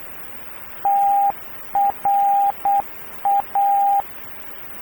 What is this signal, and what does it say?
Morse code, heard as a keyed tone, about 8 wpm, 800 Hz, with radio noise. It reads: TRA
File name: radio tv historisch / AM Sender / TRA